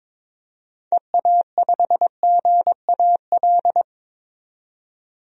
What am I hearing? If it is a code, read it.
Morse code, heard as a keyed tone, about 22 wpm, 700 Hz. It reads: EA5GAL
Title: DXCC-100 Call Signs - 6 Characters - 22wpm